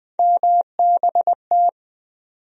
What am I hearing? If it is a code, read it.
Morse code, heard as a keyed tone, about 20 wpm, 700 Hz. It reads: MBT
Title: USA Call Sign Postfixes - 3 Letters 20wpm